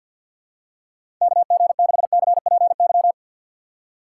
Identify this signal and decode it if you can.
Morse code, heard as a keyed tone, about 47 wpm, 700 Hz. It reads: KG6CPY
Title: USA Call Signs - 6 Characters - 6x Character Spacing - 50wpm